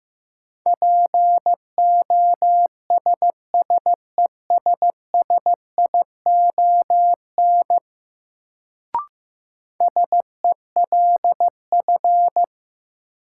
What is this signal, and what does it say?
Morse code, heard as a keyed tone, about 15 wpm, 700 Hz. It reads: POSSESSION SELF
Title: Sentences from Top 1400 Words - New Word Drill Alt - 15wpm